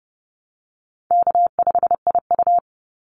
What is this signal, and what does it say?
Morse code, heard as a keyed tone, about 30 wpm, 700 Hz. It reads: K5IU
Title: Easy Call Signs Repeated 3x - 30wpm